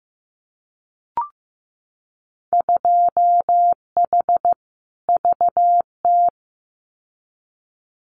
Morse code, heard as a keyed tone, about 15 wpm, 700 Hz. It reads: 2H VT
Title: Winter Field Day Exchanges 15wpm